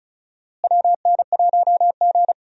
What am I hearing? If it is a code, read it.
Morse code, heard as a keyed tone, about 36 wpm, 700 Hz. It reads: WN1G
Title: USA Call Signs - 4 Characters - Repeated 3x - 35wpm